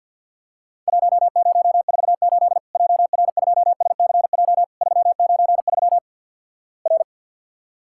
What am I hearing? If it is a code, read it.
Morse code, heard as a keyed tone, about 50 wpm, 700 Hz. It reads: 1048 JR2SCJ 393 R
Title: Worked All Europe DX Contest - QTC Exchange with Repeats - 50wpm